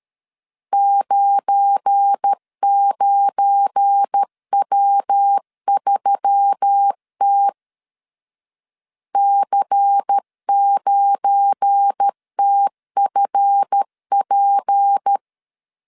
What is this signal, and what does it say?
Morse code, heard as a keyed tone, about 13 wpm, 800 Hz. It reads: 99W3T C9TFP
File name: LEZIONI / 7 GF9Z / 7 GF9 12wpm 2319